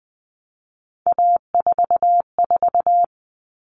Morse code, heard as a keyed tone, about 20 wpm, 700 Hz. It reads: A44
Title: All Cut Numbers - 3-Digit 20wpm